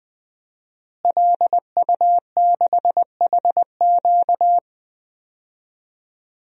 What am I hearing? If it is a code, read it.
Morse code, heard as a keyed tone, about 20 wpm, 700 Hz. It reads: LU6HQ
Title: DXCC-100 Call Signs - 5 Characters - Repeated 2x - 20wpm